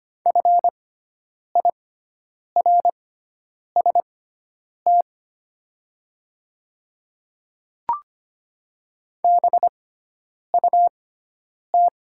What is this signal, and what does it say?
Morse code, heard as a keyed tone, about 25 wpm, spaced as for 9 wpm, 700 Hz. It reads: FIRST BUT
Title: 6x Character Spacing - Top 100 Words - 25wpm